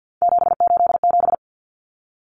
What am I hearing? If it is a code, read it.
Morse code, heard as a keyed tone, about 50 wpm, 700 Hz. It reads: 787
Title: Common Cut Numbers - 3-Digit 50wpm